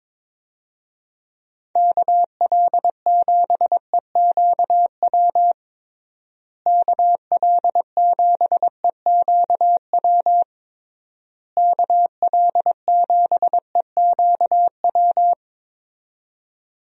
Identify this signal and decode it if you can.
Morse code, heard as a keyed tone, about 22 wpm, 700 Hz. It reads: KL7EQW KL7EQW KL7EQW
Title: DXCC-100 Call Signs - 6 Characters - Repeated 3x - 22wpm